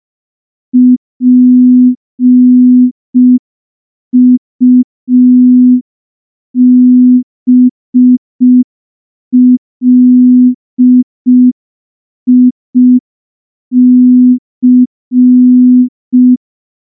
Morse code, heard as a keyed tone, about 5 wpm, 250 Hz. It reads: PUBLIC
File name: morse / morse-training-05WPM